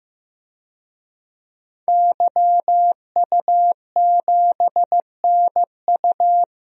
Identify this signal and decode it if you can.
Morse code, heard as a keyed tone, about 15 wpm, 700 Hz. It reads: YU7NU
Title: DXCC-100 Call Signs - 5 Characters - 15wpm